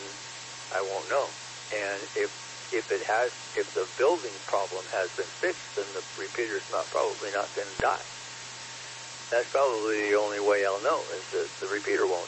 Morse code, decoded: none

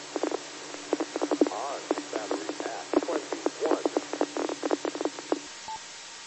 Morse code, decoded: none

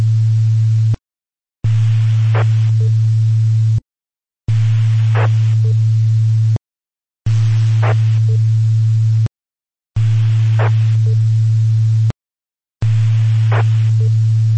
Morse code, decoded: none

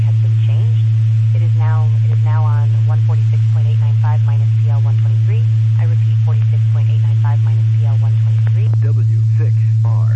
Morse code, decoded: none

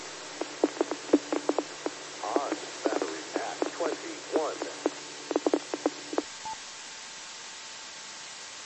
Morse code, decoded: none

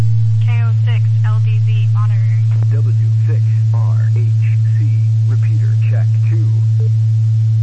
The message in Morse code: none

